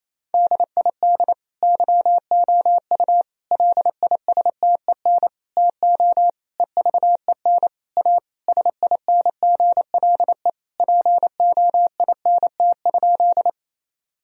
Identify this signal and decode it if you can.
Morse code, heard as a keyed tone, about 28 wpm, 700 Hz. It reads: DID YOU LISTEN TO EVEN A SINGLE POINT?
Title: Sentences from Top 1000 Words - Review Part 4 - 28wpm